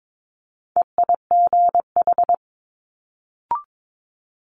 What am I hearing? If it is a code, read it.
Morse code, heard as a keyed tone, about 22 wpm, 700 Hz. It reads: EIGH
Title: Multi-letter Phonemes Quick Fire 22wpm